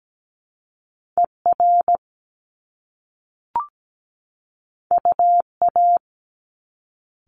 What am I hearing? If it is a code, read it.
Morse code, heard as a keyed tone, about 17 wpm, 700 Hz. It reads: ER UA
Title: Top Two Letter Combinations in Words 17wpm